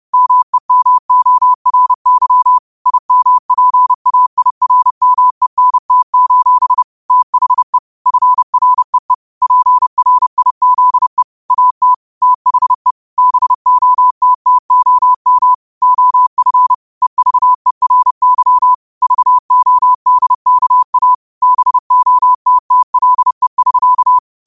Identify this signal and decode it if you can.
Morse code, heard as a keyed tone, about 30 wpm, 1000 Hz. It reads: MEMORY IMPAIRMENT: THE FREE PRIZE AT THE BOTTOM OF EVERY VODKA BOTTLE<SK>
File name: qotdfiles / 30wpm